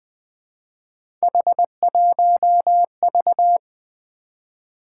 Morse code, heard as a keyed tone, about 20 wpm, 700 Hz. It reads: H1V
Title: Three Letter-Number Combinations - 20wpm